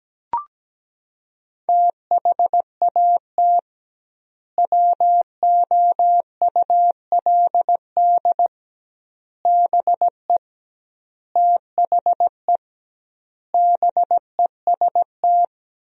Morse code, heard as a keyed tone, about 17 wpm, 700 Hz. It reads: THAT WOULD BE THE BEST
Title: 2x Word-Spacing - Sentences from Top 400 Words - 17wpm